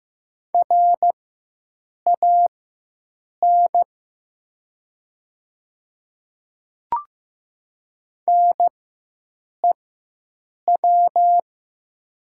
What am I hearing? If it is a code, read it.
Morse code, heard as a keyed tone, about 15 wpm, spaced as for 7 wpm, 700 Hz. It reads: RAN NEW
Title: 4x Character Spacing - Top 500 Words - 15wpm